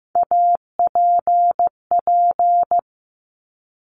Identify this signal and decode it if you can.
Morse code, heard as a keyed tone, about 15 wpm, 700 Hz. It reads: APP
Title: Top Three Letter Combinations in Words 15wpm